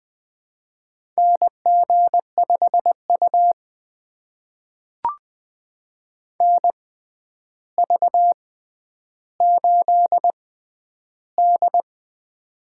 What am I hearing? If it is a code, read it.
Morse code, heard as a keyed tone, about 20 wpm, spaced as for 10 wpm, 700 Hz. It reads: NG5U NV8D
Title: USA Call Signs - 4 Characters - 6x Character Spacing - 20wpm